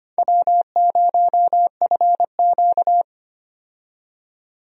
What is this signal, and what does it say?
Morse code, heard as a keyed tone, about 25 wpm, 700 Hz. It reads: W0FQ
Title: USA Call Signs - 4 Characters - 25wpm